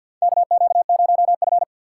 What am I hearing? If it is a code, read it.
Morse code, heard as a keyed tone, about 52 wpm, 700 Hz. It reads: KQ0F
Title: USA Call Signs - 4 Characters - 3x Character Spacing - 50wpm